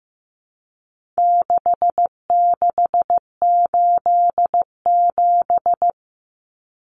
Morse code, heard as a keyed tone, about 15 wpm, 700 Hz. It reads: 6687
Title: Common Cut Numbers - 4-Digit 15wpm